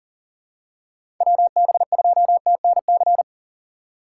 Morse code, heard as a keyed tone, about 40 wpm, 700 Hz. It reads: WB2TNC
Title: USA Call Signs - 6 Characters - 6x Character Spacing - 40wpm